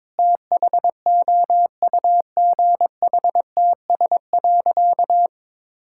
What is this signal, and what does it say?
Morse code, heard as a keyed tone, about 22 wpm, 700 Hz. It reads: THOUGHTS.
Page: morse-code-ninja-practice-22wpm